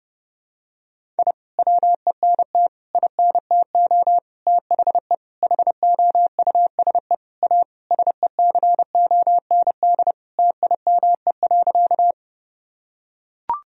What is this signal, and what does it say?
Morse code, heard as a keyed tone, about 30 wpm, 700 Hz. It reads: I WENT INTO THE HOUSE A SECOND TIME.
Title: Sentences from Top 1000 Words - Review Part 1 - 30wpm